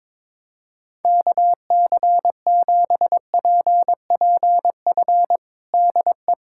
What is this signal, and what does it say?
Morse code, heard as a keyed tone, about 22 wpm, 700 Hz. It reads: KC7PPF DE
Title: USA Callsign DE Callsign - 6 Characters - 22wpm